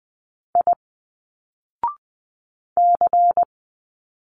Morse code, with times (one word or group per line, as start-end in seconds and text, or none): 0.5-0.7 I
2.8-3.4 C